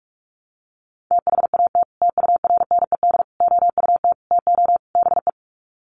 Morse code, heard as a keyed tone, about 44 wpm, 700 Hz. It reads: THAT TURNED OUT TO BE
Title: Sets of 5 Words Encore2 45wpm